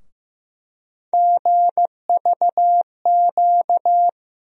Morse code, heard as a keyed tone, about 15 wpm, 700 Hz. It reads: GVQ
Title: USA Call Sign Postfixes - 3 Letters 15wpm